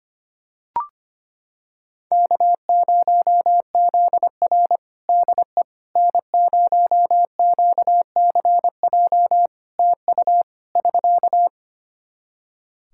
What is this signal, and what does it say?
Morse code, heard as a keyed tone, about 25 wpm, 700 Hz. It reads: K0ZR DE N0QCJ TU <SK>
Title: Practice QSO - Exchange 4 - Ending - 25wpm